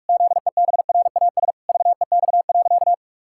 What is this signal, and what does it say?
Morse code, heard as a keyed tone, about 46 wpm, 700 Hz. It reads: ZEBRAS VEX.
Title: Pangram Challenge v2 - 45wpm